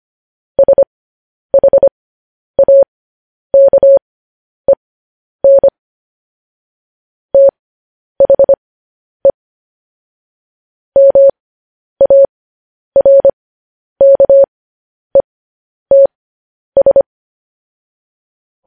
Morse code, heard as a keyed tone, about 25 wpm, spaced as for 10 wpm, 550 Hz. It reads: SHAKEN THE MARKETS
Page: news headlines in morse code at 10 wpm